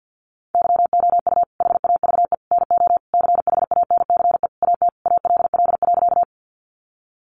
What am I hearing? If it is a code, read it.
Morse code, heard as a keyed tone, about 50 wpm, 700 Hz. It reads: YOU HAVE NO CHANCE AT ALL.
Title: Sentences from Top 800 Words 50wpm